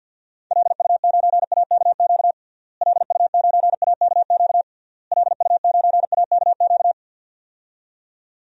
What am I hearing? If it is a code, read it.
Morse code, heard as a keyed tone, about 51 wpm, 700 Hz. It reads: LU9AKQ LU9AKQ LU9AKQ
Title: DXCC-100 Call Signs - 6 Characters - Repeated 3x - 50wpm